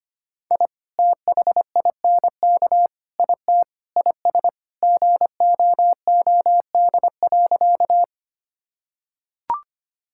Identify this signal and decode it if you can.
Morse code, heard as a keyed tone, about 25 wpm, 700 Hz. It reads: I THINK IT IS GOOD.
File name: practice / 25wpm / Sentences with 3-Word Phrases - 25wpm